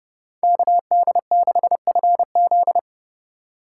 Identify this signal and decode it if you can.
Morse code, heard as a keyed tone, about 27 wpm, 700 Hz. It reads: KD6FZ
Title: Speed Racing - USA 2x2 Call Signs 30wpm